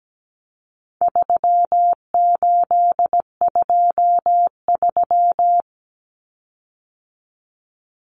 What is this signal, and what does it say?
Morse code, heard as a keyed tone, about 17 wpm, 700 Hz. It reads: 3823
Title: Common Cut Numbers - 4-Digit 17wpm